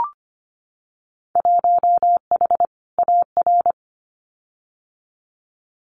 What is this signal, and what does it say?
Morse code, heard as a keyed tone, about 25 wpm, 700 Hz. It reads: 1H AR
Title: Winter Field Day Exchanges 25wpm